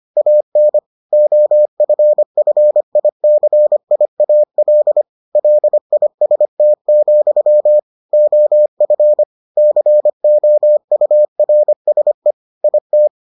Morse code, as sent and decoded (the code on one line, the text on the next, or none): .- -. / --- ..-. ..-. .. -.-. .. .- .-.. / .-.. .. ... - --..-- / --- ..-. / -.-. --- ..- .-. ... / .. -
AN OFFICIAL LIST, OF COURSE IT